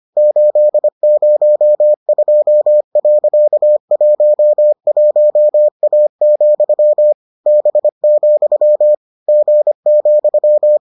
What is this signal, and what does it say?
Morse code, heard as a keyed tone, about 25 wpm, 600 Hz. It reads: 802.11A, B, G,